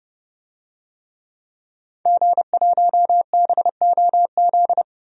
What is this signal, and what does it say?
Morse code, heard as a keyed tone, about 30 wpm, 700 Hz. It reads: G1BOZ
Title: DXCC-100 Call Signs - 5 Characters - Repeated 3x - 30wpm